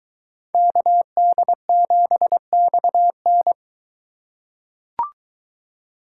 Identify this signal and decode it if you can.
Morse code, heard as a keyed tone, about 22 wpm, 700 Hz. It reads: KD7XN